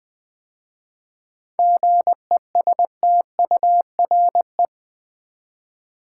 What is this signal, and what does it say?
Morse code, heard as a keyed tone, about 20 wpm, 690 Hz. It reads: GESTURE
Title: 7 Letter Words - First 3 Letters Given - 20wpm